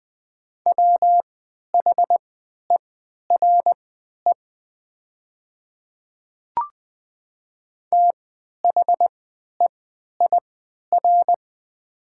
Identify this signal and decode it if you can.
Morse code, heard as a keyed tone, about 20 wpm, spaced as for 11 wpm, 690 Hz. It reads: WHERE THEIR
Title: 3x Character Spacing - Top 300 Words - 20wpm